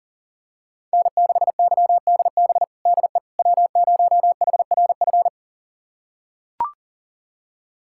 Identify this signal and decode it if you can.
Morse code, heard as a keyed tone, about 40 wpm, 700 Hz. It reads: N6YDB DE W0HRF
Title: USA Callsign DE Callsign - 5 Characters - 40wpm